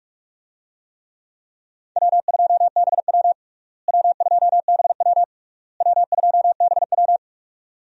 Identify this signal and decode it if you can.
Morse code, heard as a keyed tone, about 42 wpm, 700 Hz. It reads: W2BW W2BW W2BW
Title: USA Call Signs - 4 Characters - Repeated 3x - 45wpm